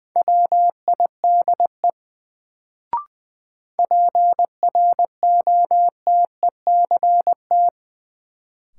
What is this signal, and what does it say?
Morse code, heard as a keyed tone, about 20 wpm, 700 Hz. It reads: WIDE PROTECT
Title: Sentences from Top 900 Words - New Word Drill - 20wpm